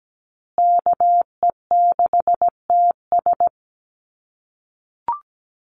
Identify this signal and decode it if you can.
Morse code, heard as a keyed tone, about 17 wpm, 700 Hz. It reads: KE6TS
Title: USA Call Signs Repeated 3x 17wpm